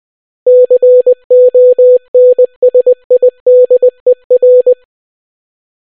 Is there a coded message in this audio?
Yes: Morse code from 0.5 to 4.7 s, heard as a keyed tone, about 20 wpm, 500 Hz. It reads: CONSIDER